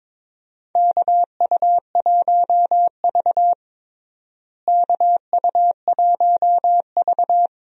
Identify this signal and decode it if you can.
Morse code, heard as a keyed tone, about 22 wpm, 700 Hz. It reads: KU1V KU1V
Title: USA Call Signs - 4 Characters - Repeated 3x - 22wpm